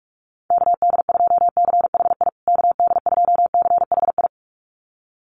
Morse code, heard as a keyed tone, about 44 wpm, 700 Hz. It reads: KD2CHI KD2CHI
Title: USA Call Signs Repeated 2x 45wpm